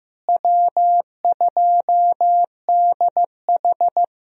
Morse code, heard as a keyed tone, about 15 wpm, 700 Hz. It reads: W2DH
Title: USA Call Signs - 4 Characters - Repeated 3x - 15wpm